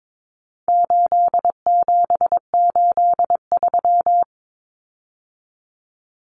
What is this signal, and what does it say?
Morse code, heard as a keyed tone, about 22 wpm, 700 Hz. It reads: 8783
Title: Common Cut Numbers - 4-Digit 22wpm